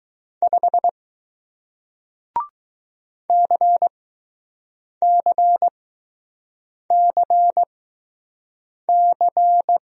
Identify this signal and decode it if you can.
Morse code, heard as a keyed tone, about 20 wpm, spaced as for 6 wpm, 700 Hz. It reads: 5 CCCC